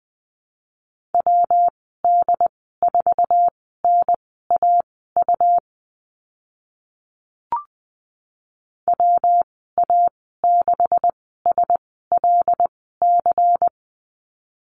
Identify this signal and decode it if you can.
Morse code, heard as a keyed tone, about 20 wpm, spaced as for 14 wpm, 700 Hz. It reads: WD4NAU WA6SLC
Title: 161 Lesson 26g - Letter g - Call Signs 20wpm